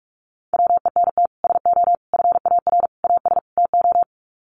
Morse code, heard as a keyed tone, about 44 wpm, 700 Hz. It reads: WENT SO FAR AS TO
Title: Sets of 5 Words Encore 45wpm